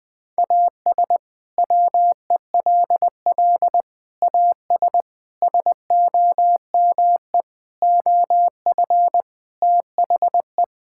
Morse code, heard as a keyed tone, about 21 wpm, 700 Hz. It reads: AS WELL AS SOME OF THE
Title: Sets of 6 Words - 20wpm